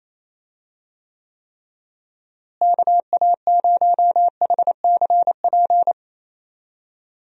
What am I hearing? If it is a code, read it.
Morse code, heard as a keyed tone, about 28 wpm, 700 Hz. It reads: KA0HCP
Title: USA Call Signs - 6 Characters - Repeated 3x - 28wpm